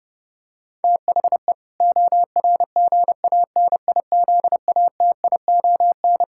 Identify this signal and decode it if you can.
Morse code, heard as a keyed tone, about 30 wpm, 700 Hz. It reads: THE ORGANIZATION
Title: Sentences from Top 1000 Words - Review Part 6 - 30wpm